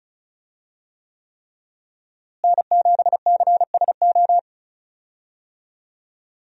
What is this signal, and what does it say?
Morse code, heard as a keyed tone, about 36 wpm, 700 Hz. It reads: N7CSO
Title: USA Call Signs - 5 Characters - Repeated 3x - 35wpm